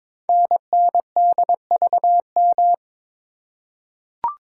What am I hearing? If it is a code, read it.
Morse code, heard as a keyed tone, about 22 wpm, 700 Hz. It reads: NNDVM
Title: Code Groups - 5 Letters - 22wpm